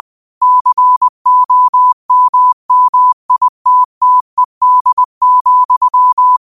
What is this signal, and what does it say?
Morse code, heard as a keyed tone, about 20 wpm, 1000 Hz. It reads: COMMITTED,
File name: qotdfiles / 20wpm